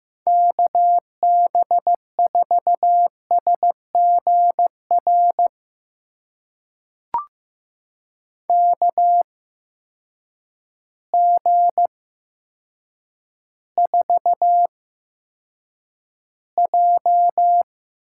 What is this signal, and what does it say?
Morse code, heard as a keyed tone, about 15 wpm, spaced as for 8 wpm, 700 Hz. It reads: KB4SGR KG4J
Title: USA Call Signs - 6 Characters - 8x Character Spacing - 15wpm